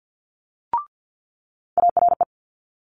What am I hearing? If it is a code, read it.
Morse code, heard as a keyed tone, about 50 wpm, 700 Hz. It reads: ARE
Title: Multi-letter Phonemes Quick Fire 50wpm